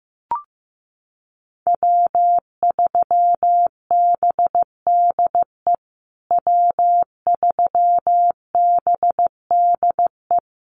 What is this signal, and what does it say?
Morse code, heard as a keyed tone, about 15 wpm, 700 Hz. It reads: W3BDE W3BDE
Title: USA Call Signs Repeated 2x 15wpm